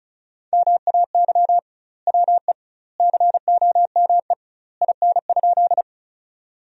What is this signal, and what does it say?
Morse code, heard as a keyed tone, about 34 wpm, 700 Hz. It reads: MAY WE COME IN?
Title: 2x Word-Spacing - Sentences from Top 100 Words - 35wpm